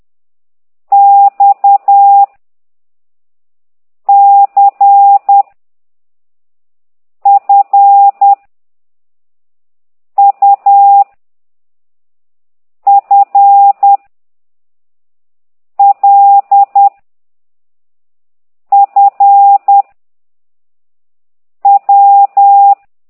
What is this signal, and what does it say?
Morse code, heard as a keyed tone, about 10 wpm, 800 Hz. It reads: X C F U F L F W
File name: ct2idw / exerc11